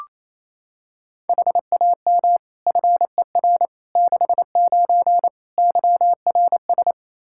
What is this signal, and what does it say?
Morse code, heard as a keyed tone, about 28 wpm, 700 Hz. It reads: HAM FER 69 YRS
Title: Practice QSO - Exchange 3 - More Info - 28wpm